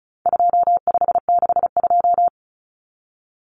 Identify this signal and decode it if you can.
Morse code, heard as a keyed tone, about 34 wpm, 700 Hz. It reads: 2562
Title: Common Cut Numbers - 4-Digit 35wpm